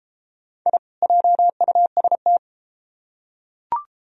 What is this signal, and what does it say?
Morse code, heard as a keyed tone, about 34 wpm, 700 Hz. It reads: I JUST